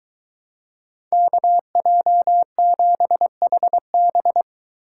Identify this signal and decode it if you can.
Morse code, heard as a keyed tone, about 22 wpm, 700 Hz. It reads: KJ7HB